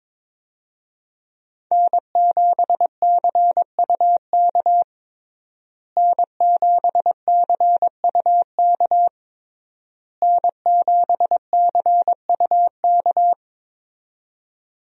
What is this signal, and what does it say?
Morse code, heard as a keyed tone, about 22 wpm, 700 Hz. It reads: N7CUK N7CUK N7CUK
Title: USA Call Signs - 5 Characters - Repeated 3x - 22wpm